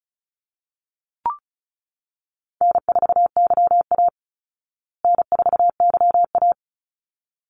Morse code, heard as a keyed tone, about 34 wpm, 700 Hz. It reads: N4YA N4YA
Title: Call Signs Repeated 3x - 35wpm